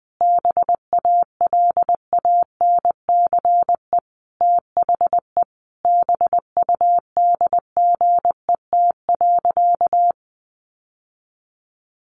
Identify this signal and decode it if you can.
Morse code, heard as a keyed tone, about 20 wpm, 700 Hz. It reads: BALANCE THE BUDGET.
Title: Sentences from Top 1200 Words 20wpm